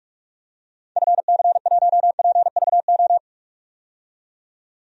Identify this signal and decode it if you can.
Morse code, heard as a keyed tone, about 46 wpm, 700 Hz. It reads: FC1PVO
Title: DXCC-100 Call Signs - 6 Characters - 45wpm